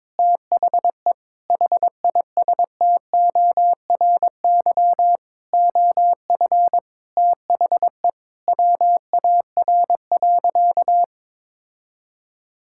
Morse code, heard as a keyed tone, about 22 wpm, 700 Hz. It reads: THE HISTORY OF THE WAR.
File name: practice / 22wpm / Sentences with 4-Word Phrases - 22wpm